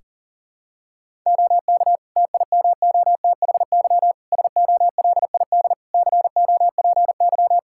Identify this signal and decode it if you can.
Morse code, heard as a keyed tone, about 40 wpm, 700 Hz. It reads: OK TIMOTHY SOLID COPY
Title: K1USN SST QSO Practice 40wpm